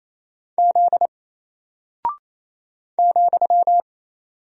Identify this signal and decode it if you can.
Morse code, heard as a keyed tone, about 28 wpm, 700 Hz. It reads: Z ,
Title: Single Character (Letter:Number:Punctuation:Prosign) Alt 28wpm